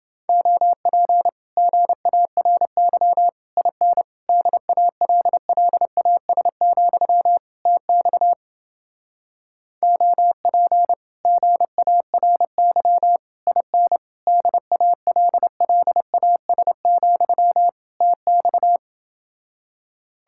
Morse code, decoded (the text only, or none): OP GARY IN DALLAS, TX OP GARY IN DALLAS, TX